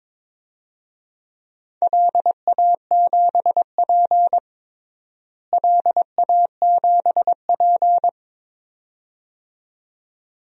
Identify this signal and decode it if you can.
Morse code, heard as a keyed tone, about 22 wpm, 700 Hz. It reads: LA7P LA7P
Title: DXCC-100 Call Signs - 4 Characters - Repeated 2x - 22wpm